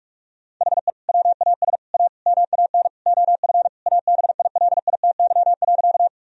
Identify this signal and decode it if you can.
Morse code, heard as a keyed tone, about 42 wpm, 700 Hz. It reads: HE WAS A MAN OF ABILITY.
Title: Sentences from Top 1000 Words - Review Part 3 - 45wpm